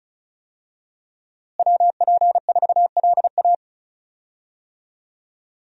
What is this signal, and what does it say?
Morse code, heard as a keyed tone, about 36 wpm, 700 Hz. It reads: WP4LA
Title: USA Call Signs - 5 Characters - Repeated 3x - 35wpm